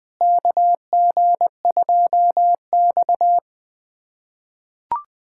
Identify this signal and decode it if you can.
Morse code, heard as a keyed tone, about 21 wpm, 700 Hz. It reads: KG2X
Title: USA Callsign DE Callsign - 4 Characters - 20wpm